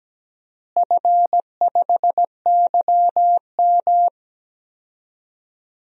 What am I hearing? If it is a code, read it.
Morse code, heard as a keyed tone, about 17 wpm, 700 Hz. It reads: F5YM
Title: DXCC-100 Call Signs - 4 Characters - Repeated 2x - 17wpm